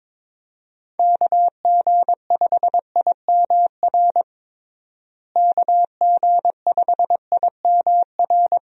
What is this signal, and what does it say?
Morse code, heard as a keyed tone, about 22 wpm, 700 Hz. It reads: KG5IMR KG5IMR
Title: USA Call Signs - 6 Characters - Repeated 2x - 22wpm